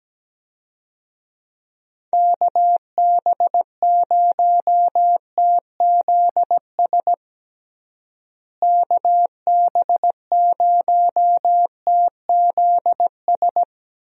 Morse code, heard as a keyed tone, about 17 wpm, 700 Hz. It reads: KB0TZS KB0TZS
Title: USA Call Signs - 6 Characters - Repeated 3x - 17wpm